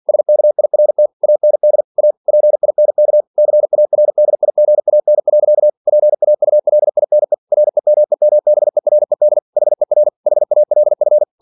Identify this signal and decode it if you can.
Morse code, heard as a keyed tone, about 49 wpm, 600 Hz. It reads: SKIRT AND A PINK CARDIGAN. PAULINE REMEMBERED HER HALF
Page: Never 47wpm